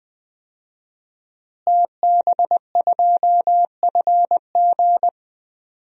Morse code, heard as a keyed tone, about 20 wpm, 700 Hz. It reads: TB2FG
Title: DXCC-100 Call Signs - 5 Characters - Repeated 3x - 20wpm